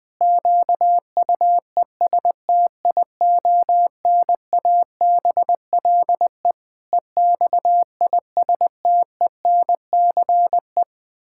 Morse code, heard as a keyed tone, about 20 wpm, 700 Hz. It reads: QUESTIONABLE EXISTENCE
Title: Pangram Challenge v2 - 20wpm